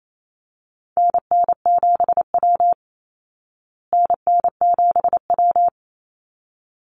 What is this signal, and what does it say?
Morse code, heard as a keyed tone, about 28 wpm, 700 Hz. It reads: NN7W NN7W
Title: Call Signs Repeated 3x - 28wpm